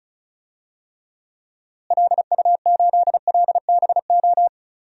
Morse code, heard as a keyed tone, about 36 wpm, 700 Hz. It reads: LU8LBO
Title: DXCC-100 Call Signs - 6 Characters - Repeated 2x - 35wpm